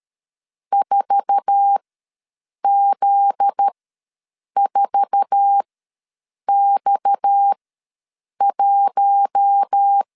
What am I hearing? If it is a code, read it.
Morse code, heard as a keyed tone, about 13 wpm, spaced as for 7 wpm, 800 Hz. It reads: 4Z4X1
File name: LEZIONI / 8 YJ45 / 8 YJ4 ws 12wpm 3249